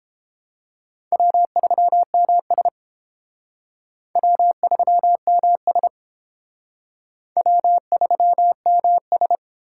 Morse code, heard as a keyed tone, about 30 wpm, 700 Hz. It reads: W3MS W3MS W3MS